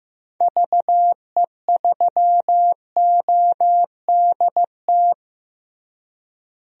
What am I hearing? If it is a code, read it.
Morse code, heard as a keyed tone, about 15 wpm, 700 Hz. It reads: VE3ODT
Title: DXCC-100 Call Signs - 6 Characters - Repeated 3x - 15wpm